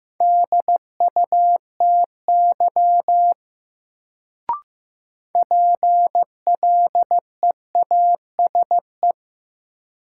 Morse code, heard as a keyed tone, about 15 wpm, 700 Hz. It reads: DUTY PLEASE
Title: Sentences from Top 1500 Words - New Word Drill Alt - 15wpm